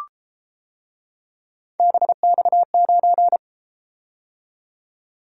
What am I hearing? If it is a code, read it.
Morse code, heard as a keyed tone, about 31 wpm, 700 Hz. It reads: BX9